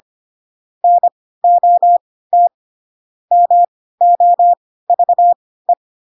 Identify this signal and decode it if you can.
Morse code, heard as a keyed tone, about 25 wpm, spaced as for 16 wpm, 700 Hz. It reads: NOT MOVE
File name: Pinocchio / Pinocchio ch3 10 25 ik1hge